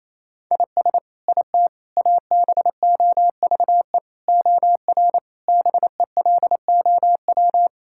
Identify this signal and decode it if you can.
Morse code, heard as a keyed tone, about 28 wpm, 700 Hz. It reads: IS IT ABOVE OR BELOW
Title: Sentences from Top 1000 Words - Review Part 5 - 28wpm